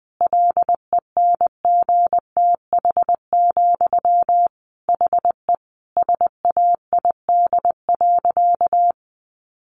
Morse code, heard as a keyed tone, about 20 wpm, 700 Hz. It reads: LENGTH, HE SAID.
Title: Sentences from Top 1000 Words 20wpm